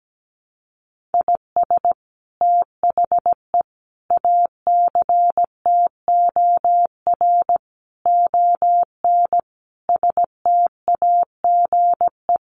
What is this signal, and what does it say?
Morse code, heard as a keyed tone, about 17 wpm, 700 Hz. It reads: IS THE ACTOR ON STAGE
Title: Sentences from Top 1500 Words 17wpm